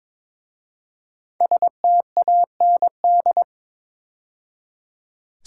Morse code, heard as a keyed tone, about 23 wpm, 700 Hz. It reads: STAND
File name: 5 Letter Words - First 3 Letters Given - 22wpm